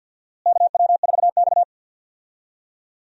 Morse code, heard as a keyed tone, about 52 wpm, 700 Hz. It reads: KW4X
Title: USA Call Signs - 4 Characters - Repeated 2x - 50wpm